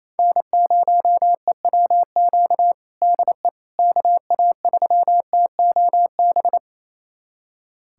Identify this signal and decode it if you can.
Morse code, heard as a keyed tone, about 28 wpm, 700 Hz. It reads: N0EWQ DE KA3TOB
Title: USA Callsign DE Callsign 28wpm